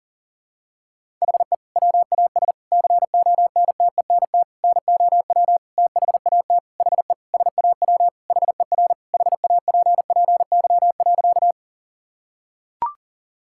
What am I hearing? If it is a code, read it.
Morse code, heard as a keyed tone, about 40 wpm, 700 Hz. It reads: HE WAS CONTENT NOW THAT HE SAW HER HAPPY.
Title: Sentences from Top 1000 Words - Review Part 6 - 40wpm